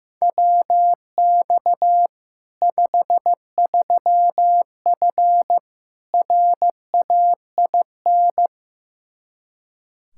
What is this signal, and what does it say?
Morse code, heard as a keyed tone, about 15 wpm, 700 Hz. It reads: WX 53F RAIN
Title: Practice QSO - Exchange 2 - Info - 15wpm